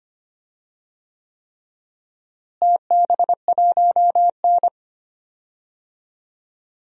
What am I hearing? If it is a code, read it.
Morse code, heard as a keyed tone, about 25 wpm, 700 Hz. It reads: TB1N
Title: DXCC-100 Call Signs - 4 Characters - Repeated 3x - 25wpm